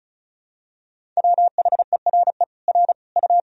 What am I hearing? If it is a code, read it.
Morse code, heard as a keyed tone, about 34 wpm, 700 Hz. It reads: WHERE R U
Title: Sentences from Top 200 Words and Abbreviations 35wpm